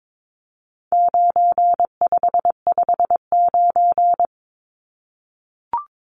Morse code, heard as a keyed tone, about 22 wpm, 700 Hz. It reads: 9559
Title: Numbers - 4-Digit 22wpm